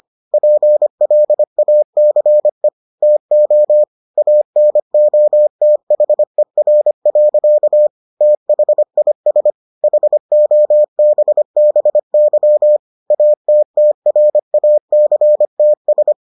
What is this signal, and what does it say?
Morse code, heard as a keyed tone, about 25 wpm, 600 Hz. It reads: PLACE TO ANOTHER. THIS HOBBY ATTRACTS